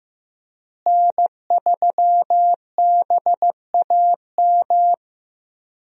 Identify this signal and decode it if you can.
Morse code, heard as a keyed tone, about 15 wpm, 700 Hz. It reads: N3BAM
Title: USA Call Signs - 5 Characters - Repeated 2x - 15wpm